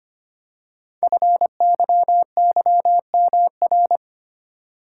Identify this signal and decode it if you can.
Morse code, heard as a keyed tone, about 25 wpm, 700 Hz. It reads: FYYMR
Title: Code Groups - 5 Letter:Numbers - 25wpm